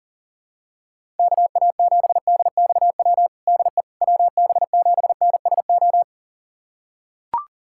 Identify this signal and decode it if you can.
Morse code, heard as a keyed tone, about 37 wpm, 700 Hz. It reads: KA7DXW DE WB7NSO
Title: USA Callsign DE Callsign 40wpm